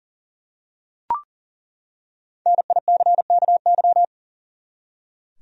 Morse code, heard as a keyed tone, about 40 wpm, 700 Hz. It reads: NICKY
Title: USA Common Names - 40wpm